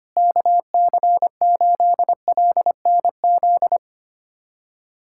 Morse code, heard as a keyed tone, about 25 wpm, 700 Hz. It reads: KC8LNZ
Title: USA Callsign DE Callsign - 6 Characters - 25wpm